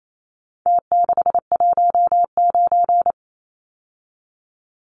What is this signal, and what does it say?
Morse code, heard as a keyed tone, about 28 wpm, 700 Hz. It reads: T619
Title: Numbers - Contest Style - 4-Digit 28wpm